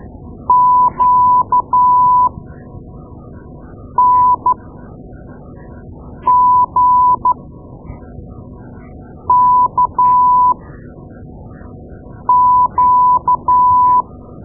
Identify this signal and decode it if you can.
Morse code, heard as a keyed tone, about 12 wpm, spaced as for 4 wpm, 1000 Hz. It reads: QNGKQ
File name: Morse Lessons / Morse Lessons / Tape-2A